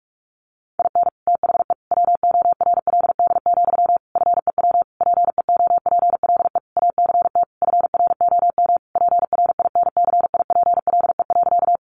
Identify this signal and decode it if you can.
Morse code, heard as a keyed tone, about 44 wpm, 700 Hz. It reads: IN THE WORLD, FEW PEOPLE ACT FROM PRINCIPLE.
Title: Sentences from Top 1400 Words 45wpm